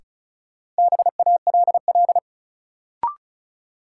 Morse code, heard as a keyed tone, about 36 wpm, 700 Hz. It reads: BALL
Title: Sentences from Top 900 Words - New Word Drill - 35wpm